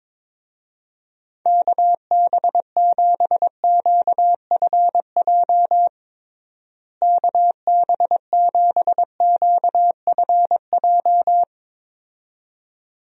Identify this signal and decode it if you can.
Morse code, heard as a keyed tone, about 22 wpm, 700 Hz. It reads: KB7QFJ KB7QFJ
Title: USA Call Signs - 6 Characters - Repeated 2x - 22wpm